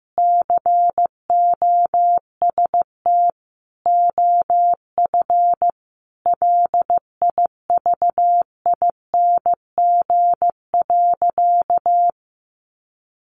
Morse code, heard as a keyed tone, about 15 wpm, 700 Hz. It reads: COST OF LIVING.
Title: Sentences from Top 1400 Words 15wpm